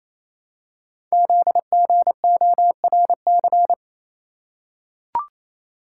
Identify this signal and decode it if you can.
Morse code, heard as a keyed tone, about 27 wpm, 700 Hz. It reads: ZGORC
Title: Code Groups - 5 Letters - 28wpm